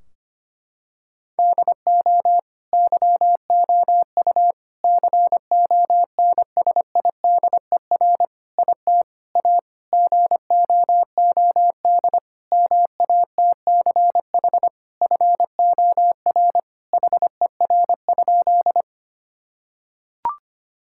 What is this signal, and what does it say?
Morse code, heard as a keyed tone, about 25 wpm, 700 Hz. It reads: DO YOU CONSIDER IT A GOOD MATCH FOR HER?
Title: Sentences from Top 1600 Words 25wpm